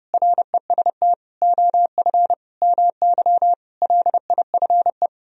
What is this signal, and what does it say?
Morse code, heard as a keyed tone, about 32 wpm, 700 Hz. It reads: REST OF MY LIFE